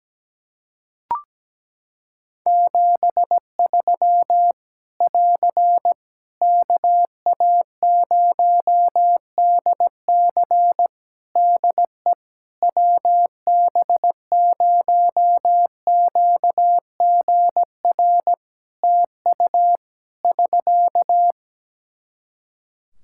Morse code, heard as a keyed tone, about 17 wpm, 700 Hz. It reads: 73 <AR> KA0DC DE WB0QGR TU <SK>
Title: Practice QSO - Exchange 4 - Ending - 17wpm